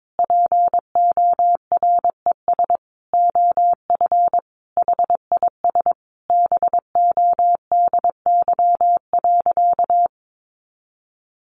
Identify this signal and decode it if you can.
Morse code, heard as a keyed tone, about 22 wpm, 700 Hz. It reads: PORES OF HIS BODY.
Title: A Sound Of Thunder 22wpm